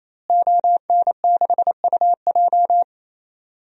Morse code, heard as a keyed tone, about 28 wpm, 700 Hz. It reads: ON6UJ
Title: DXCC-100 Call Signs - 5 Characters - Repeated 3x - 28wpm